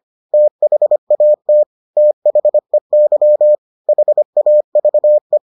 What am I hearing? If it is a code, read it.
Morse code, heard as a keyed tone, about 25 wpm, 600 Hz. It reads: THAT THEY HAVE